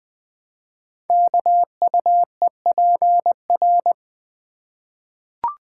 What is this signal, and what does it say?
Morse code, heard as a keyed tone, about 20 wpm, 700 Hz. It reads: KUEPR
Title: Code Groups - 5 Letters - 20wpm